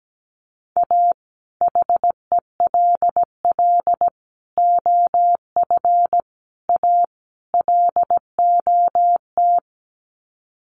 Morse code, heard as a keyed tone, about 17 wpm, 700 Hz. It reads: A HELL OF A LOT
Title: Sets of 5 Words Encore 17wpm